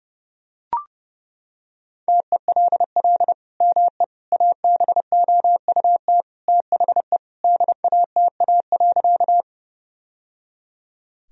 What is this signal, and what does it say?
Morse code, heard as a keyed tone, about 30 wpm, 700 Hz. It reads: TELL ME ABOUT THE DATA.
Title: Sentences from Top 1000 Words - Review Part 3 - 30wpm